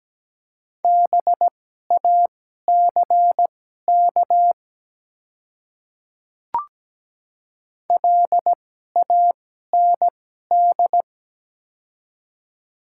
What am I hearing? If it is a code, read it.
Morse code, heard as a keyed tone, about 17 wpm, spaced as for 12 wpm, 700 Hz. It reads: BACK LAND
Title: 2x Character Spacing - Top 200 Words - 17wpm